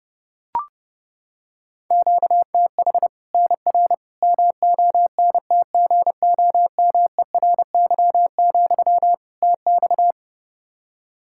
Morse code, heard as a keyed tone, about 30 wpm, 700 Hz. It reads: QTH NR MONTGOMERY, TX
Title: Practice QSO - Exchange 1 - Beginning - 30wpm